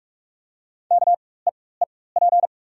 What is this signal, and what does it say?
Morse code, heard as a keyed tone, about 43 wpm, spaced as for 21 wpm, 700 Hz. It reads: KEEP
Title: 4x Character Spacing - Top 300 Words - 45wpm